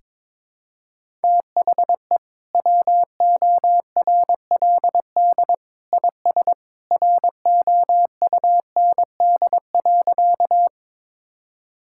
Morse code, heard as a keyed tone, about 22 wpm, 700 Hz. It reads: THE WORLD IS ROUND.
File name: practice / 22wpm / Sentences with 2-Word Phrases - 22wpm